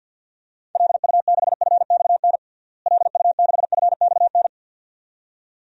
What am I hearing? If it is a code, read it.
Morse code, heard as a keyed tone, about 52 wpm, 700 Hz. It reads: LU6FXN LU6FXN
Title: DXCC-100 Call Signs - 6 Characters - Repeated 2x - 50wpm